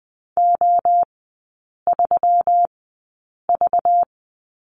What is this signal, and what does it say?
Morse code, heard as a keyed tone, about 20 wpm, spaced as for 8 wpm, 700 Hz. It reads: O3V
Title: 205 Lesson 33e - Letter j - Three Characters 20wpm